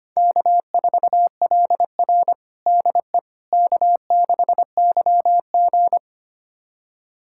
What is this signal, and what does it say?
Morse code, heard as a keyed tone, about 25 wpm, 700 Hz. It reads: K4LR DE K6YG
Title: USA Callsign DE Callsign - 4 Characters - 25wpm